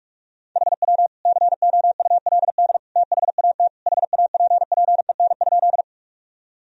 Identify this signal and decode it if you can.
Morse code, heard as a keyed tone, about 41 wpm, 700 Hz. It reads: HW COULD THAT HAPPEN?
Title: Sentences from Top 400 Words and Abbreviations 45wpm